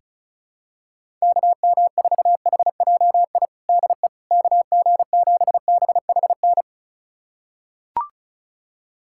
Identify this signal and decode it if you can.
Morse code, heard as a keyed tone, about 34 wpm, 700 Hz. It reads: KM4HJI DE KG7BHN
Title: USA Callsign DE Callsign - 6 Characters - 35wpm